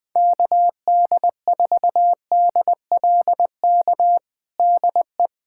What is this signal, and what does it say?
Morse code, heard as a keyed tone, about 20 wpm, 700 Hz. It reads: KD4DLK DE
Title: USA Callsign DE Callsign - 6 Characters - 20wpm